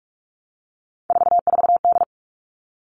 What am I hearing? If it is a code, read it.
Morse code, heard as a keyed tone, about 44 wpm, 700 Hz. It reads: 44D
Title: All Cut Numbers - 3-Digit 45wpm